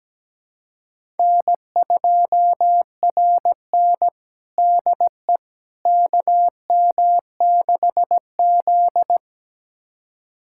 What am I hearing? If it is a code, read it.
Morse code, heard as a keyed tone, about 17 wpm, 700 Hz. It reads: N2RN DE KM6Z
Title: USA Callsign DE Callsign - 4 Characters - 17wpm